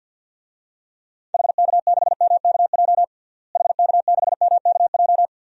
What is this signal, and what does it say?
Morse code, heard as a keyed tone, about 53 wpm, 700 Hz. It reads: HK6MKJ HK6MKJ
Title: DXCC-100 Call Signs - 6 Characters - Repeated 3x - 50wpm